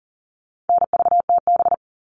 Morse code, heard as a keyed tone, about 39 wpm, 700 Hz. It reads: NVTB
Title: All Cut Numbers - 4-Digit 40wpm